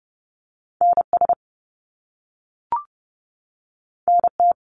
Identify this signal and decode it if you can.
Morse code, heard as a keyed tone, about 30 wpm, 700 Hz. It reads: NS NT
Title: Canadian Provinces and Territories 30wpm